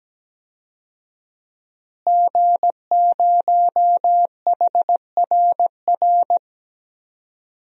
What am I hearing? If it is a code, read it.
Morse code, heard as a keyed tone, about 17 wpm, 700 Hz. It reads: G0HRR
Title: DXCC-100 Call Signs - 5 Characters - Repeated 3x - 17wpm